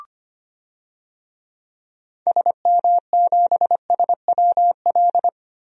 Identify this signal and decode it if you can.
Morse code, heard as a keyed tone, about 25 wpm, 700 Hz. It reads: SM7SWL
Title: DXCC-100 Call Signs - 6 Characters - Repeated 2x - 25wpm